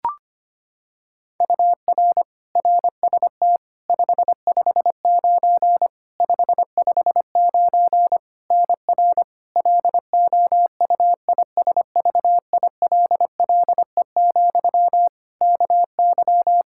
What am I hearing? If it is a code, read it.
Morse code, heard as a keyed tone, about 25 wpm, 700 Hz. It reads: UR RST 559 559 NR LOUISVILLE, KY
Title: Practice QSO - Exchange 1 - Beginning - 25wpm